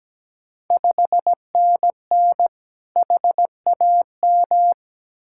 Morse code, heard as a keyed tone, about 17 wpm, 700 Hz. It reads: 5NN HAM
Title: Ontario QSO Party Exchange 17wpm